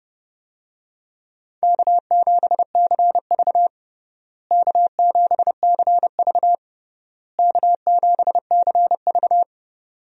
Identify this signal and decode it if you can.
Morse code, heard as a keyed tone, about 30 wpm, 700 Hz. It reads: K7CV K7CV K7CV
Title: USA Call Signs - 4 Characters - Repeated 3x - 30wpm